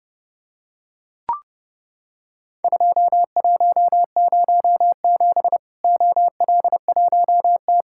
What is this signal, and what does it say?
Morse code, heard as a keyed tone, about 30 wpm, 700 Hz. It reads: 2107 OL1T
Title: Worked All Europe DX Contest - QTC Exchange - 30wpm